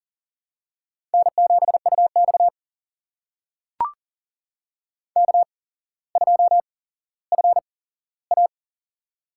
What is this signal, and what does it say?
Morse code, heard as a keyed tone, about 44 wpm, spaced as for 17 wpm, 700 Hz. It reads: N7UX K2FA
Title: USA Call Signs - 4 Characters - 8x Character Spacing - 40wpm